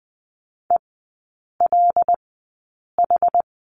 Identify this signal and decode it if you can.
Morse code, heard as a keyed tone, about 20 wpm, spaced as for 8 wpm, 700 Hz. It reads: ELH
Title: 074 Lesson 13e - Letter L - Three Characters 20wpm